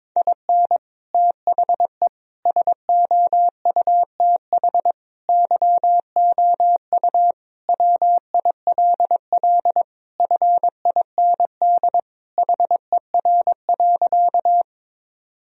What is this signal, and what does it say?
Morse code, heard as a keyed tone, about 22 wpm, 700 Hz. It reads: IN THE SOUTH YOU WILL FIND HER.
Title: Sentences from Top 1000 Words - Review Part 2 - 22wpm